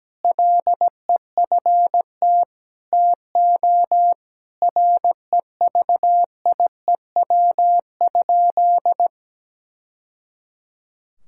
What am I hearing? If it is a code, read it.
Morse code, heard as a keyed tone, about 17 wpm, 700 Hz. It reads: LEFT TO REVIEW?
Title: Sentences from Top 1000 Words - Review Part 6 - 17wpm